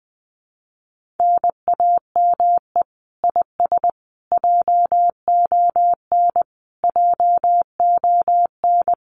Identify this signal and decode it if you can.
Morse code, heard as a keyed tone, about 20 wpm, 700 Hz. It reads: NAME IS JON JON
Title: Common QSO Phrases Encore 20wpm